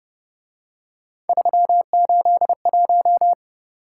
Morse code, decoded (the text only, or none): 381